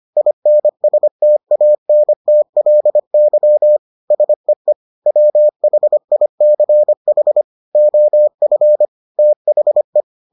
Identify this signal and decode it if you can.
Morse code, heard as a keyed tone, about 25 wpm, 600 Hz. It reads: INSTANTLY SEE WHICH OF THE